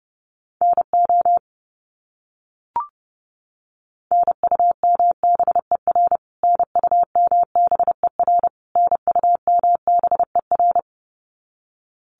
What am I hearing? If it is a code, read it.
Morse code, heard as a keyed tone, about 30 wpm, 700 Hz. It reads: NO NUMBER NUMBER NUMBER
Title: Top 300 Words Repeated 30wpm